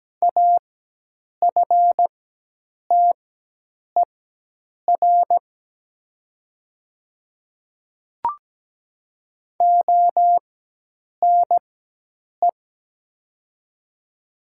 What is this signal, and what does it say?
Morse code, heard as a keyed tone, about 17 wpm, spaced as for 8 wpm, 700 Hz. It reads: AFTER ONE
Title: 4x Character Spacing - Top 200 Words - 17wpm